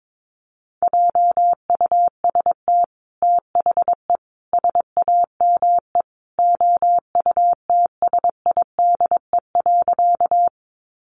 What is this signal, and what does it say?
Morse code, heard as a keyed tone, about 22 wpm, 700 Hz. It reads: JUST THE SAME OUTSIDE.
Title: Sentences from Top 900 Words 22wpm